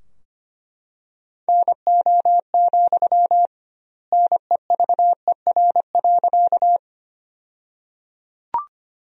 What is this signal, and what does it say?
Morse code, heard as a keyed tone, about 25 wpm, 700 Hz. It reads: NO, NEVER.
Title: 2x Word-Spacing - Sentences from Top 300 Words - 25wpm